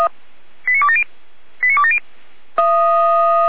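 Five-tone selective call (ZVEI), heard as quick runs of five tones, tones of about 70 ms. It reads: 89270 89270
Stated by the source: ZVEI five-tone selective call received with an SDR.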